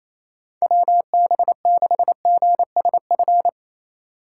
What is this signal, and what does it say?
Morse code, heard as a keyed tone, about 34 wpm, 700 Hz. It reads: WB6GSF
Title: Speed Racing - USA Call Signs 28wpm